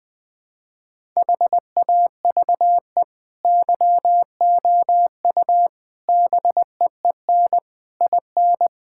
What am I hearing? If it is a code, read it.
Morse code, heard as a keyed tone, about 20 wpm, 700 Hz. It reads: HAVE YOU BEEN IN
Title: Sentences from Top 1000 Words - Review Part 5 - 20wpm